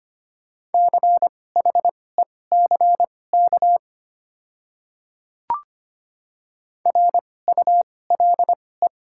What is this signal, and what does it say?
Morse code, heard as a keyed tone, about 25 wpm, spaced as for 18 wpm, 700 Hz. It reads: CHECK RULE
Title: 2x Character Spacing - Top 500 Words - 25wpm